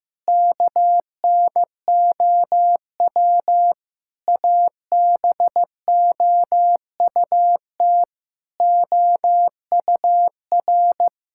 Morse code, heard as a keyed tone, about 15 wpm, 700 Hz. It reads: KNOW ABOUT OUR
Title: Sentences from Top 1000 Words - Review Part 6 - 15wpm